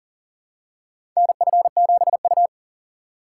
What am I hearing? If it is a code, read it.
Morse code, heard as a keyed tone, about 40 wpm, 700 Hz. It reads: NF7U
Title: USA Call Signs - 2x Character Spacing - 40wpm